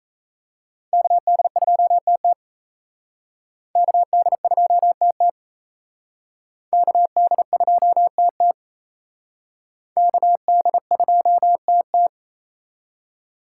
Morse code, heard as a keyed tone, about 34 wpm, 700 Hz. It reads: KD2TT KD2TT KD2TT KD2TT
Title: Speed Racing - USA Call Signs 28wpm